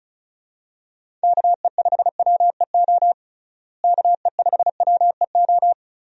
Morse code, heard as a keyed tone, about 36 wpm, 700 Hz. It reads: KE5WEO KE5WEO
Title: USA Call Signs - 6 Characters - Repeated 2x - 35wpm